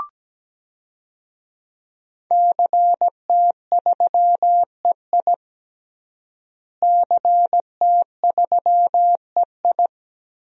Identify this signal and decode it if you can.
Morse code, heard as a keyed tone, about 17 wpm, 700 Hz. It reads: CT3EI CT3EI
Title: DXCC-100 Call Signs - 5 Characters - Repeated 2x - 17wpm